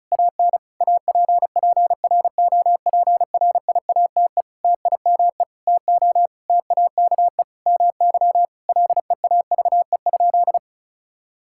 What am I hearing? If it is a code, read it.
Morse code, heard as a keyed tone, about 34 wpm, 700 Hz. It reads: AN APPROPRIATE TIME TO TAKE MY LEAVE?
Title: Sentences from Top 1000 Words - Review Part 5 - 35wpm